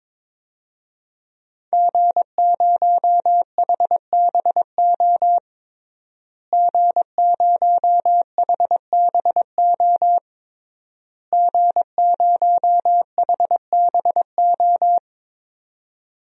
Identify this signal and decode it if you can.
Morse code, heard as a keyed tone, about 22 wpm, 700 Hz. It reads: G0HBO G0HBO G0HBO
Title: DXCC-100 Call Signs - 5 Characters - Repeated 3x - 22wpm